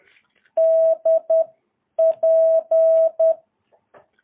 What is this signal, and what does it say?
Morse code, heard as a keyed tone, about 10 wpm, spaced as for 8 wpm, 650 Hz. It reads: DP
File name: jh8cbh / 2022cw